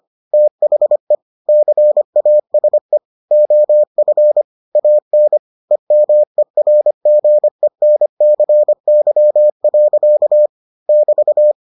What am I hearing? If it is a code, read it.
Morse code, heard as a keyed tone, about 25 wpm, 600 Hz. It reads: THE CASE OF AN EMERGENCY. <BT>